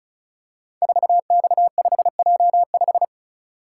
Morse code, heard as a keyed tone, about 36 wpm, 700 Hz. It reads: 4X5J5
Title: Code Groups - 5 Letter:Numbers - 35wpm